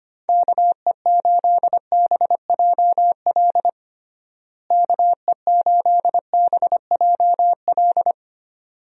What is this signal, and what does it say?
Morse code, heard as a keyed tone, about 25 wpm, 700 Hz. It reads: KE8BJL KE8BJL
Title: USA Call Signs - 6 Characters - Repeated 2x - 25wpm